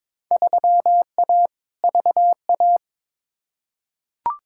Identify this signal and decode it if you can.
Morse code, heard as a keyed tone, about 22 wpm, 700 Hz. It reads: 3A VA
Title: Field Day Exchanges Encore - 22wpm